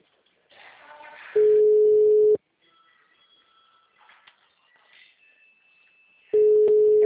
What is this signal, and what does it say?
Ringback tone (Europe, 425 Hz).